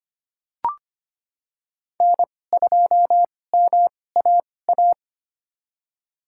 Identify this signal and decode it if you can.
Morse code, heard as a keyed tone, about 25 wpm, spaced as for 18 wpm, 700 Hz. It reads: N2MAA
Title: USA Call Signs - 5 Characters - 2x Character Spacing - 25wpm